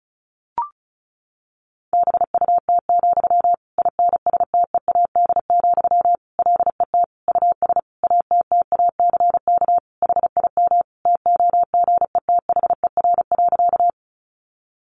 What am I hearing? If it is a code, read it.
Morse code, heard as a keyed tone, about 34 wpm, 700 Hz. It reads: BUT, INSTEAD, LET US ATTACK HIM TOGETHER.
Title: Sentences from Top 1500 Words 35wpm